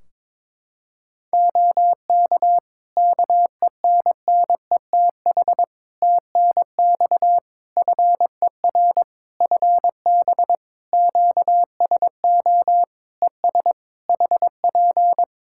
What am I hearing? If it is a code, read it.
Morse code, heard as a keyed tone, about 23 wpm, 700 Hz. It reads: OK KENNETH TNX FER FB QSO ES HP
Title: Practice QSO - Exchange 4 - Ending - 22wpm